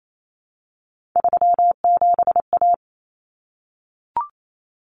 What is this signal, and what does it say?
Morse code, heard as a keyed tone, about 28 wpm, 700 Hz. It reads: 37A
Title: Common Cut Numbers - 3-Digit 28wpm